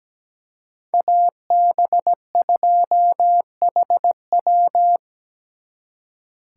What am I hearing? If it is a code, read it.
Morse code, heard as a keyed tone, about 20 wpm, 700 Hz. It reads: AB2HW
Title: Speed Racing - USA 2x2 Call Signs 17wpm